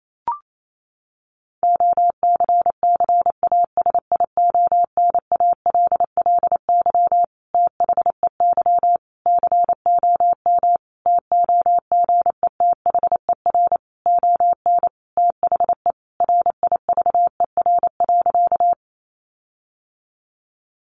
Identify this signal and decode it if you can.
Morse code, heard as a keyed tone, about 28 wpm, 700 Hz. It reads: OCCASIONALLY THEY COM TOGETHER ON THE RIVER.
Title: Sentences from Top 1300 Words 28wpm